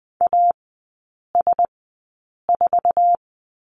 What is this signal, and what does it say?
Morse code, heard as a keyed tone, about 20 wpm, spaced as for 8 wpm, 700 Hz. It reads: AS4
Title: 107 Lesson 18e - Letter m - Three Characters 20wpm